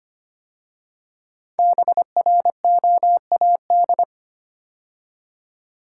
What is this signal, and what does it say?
Morse code, heard as a keyed tone, about 25 wpm, 700 Hz. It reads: BROAD
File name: 5 Letter Words - First 3 Letters Given - 25wpm